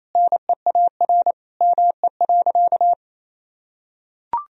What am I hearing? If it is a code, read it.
Morse code, heard as a keyed tone, about 28 wpm, 700 Hz. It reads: NEAR ME.
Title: Sentences from Top 1000 Words - Review Part 1 - 28wpm